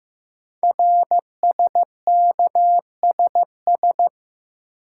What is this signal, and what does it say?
Morse code, heard as a keyed tone, about 15 wpm, 700 Hz. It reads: RSKSS
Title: Code Groups - 5 Letters - 15wpm